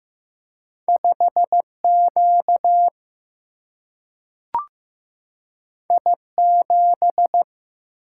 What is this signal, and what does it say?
Morse code, heard as a keyed tone, about 15 wpm, 700 Hz. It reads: 5 Q I 7
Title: Single-Letter Single-Number 15wpm